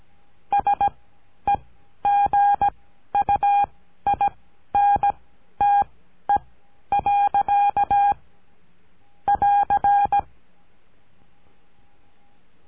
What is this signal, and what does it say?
Morse code, heard as a keyed tone, about 17 wpm, spaced as for 11 wpm, 810 Hz. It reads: SEGUINTE. <AR>
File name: CURSO DE TELEGRAFIA